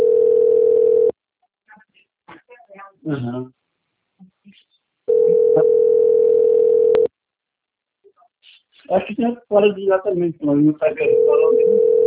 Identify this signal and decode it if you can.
Ringback tone (North America), 440 + 480 Hz, with speech.
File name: out- - - - - .51193